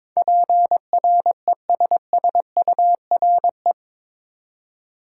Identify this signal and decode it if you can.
Morse code, heard as a keyed tone, about 22 wpm, 700 Hz. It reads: PRESSURE
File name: practice / 22wpm / Eight Letter Words - 22wpm